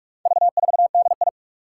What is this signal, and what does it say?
Morse code, heard as a keyed tone, about 46 wpm, 700 Hz. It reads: V4DI
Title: Four Challenging Letter-Number Combinations 45wpm